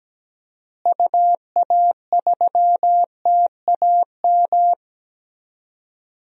Morse code, heard as a keyed tone, about 17 wpm, 700 Hz. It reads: UA3TAM
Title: DXCC-100 Call Signs - 6 Characters - Repeated 2x - 17wpm